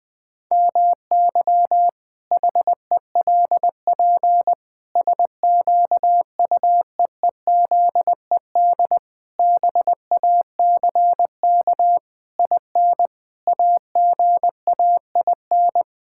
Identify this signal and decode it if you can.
Morse code, heard as a keyed tone, about 20 wpm, 700 Hz. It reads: MY HELP SQUEEZED BACK IN AGAIN
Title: Pangram Challenge v2 - 20wpm